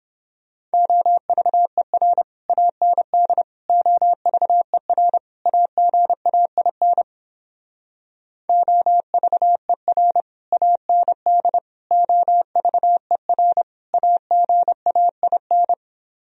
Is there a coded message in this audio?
Yes: Morse code from 0.7 to 15.7 s, heard as a keyed tone, about 31 wpm, 700 Hz. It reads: OVER AND OVER AGAIN OVER AND OVER AGAIN